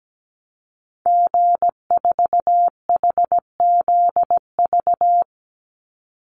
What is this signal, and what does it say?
Morse code, heard as a keyed tone, about 17 wpm, 700 Hz. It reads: G4HZV
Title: Call Signs - 17wpm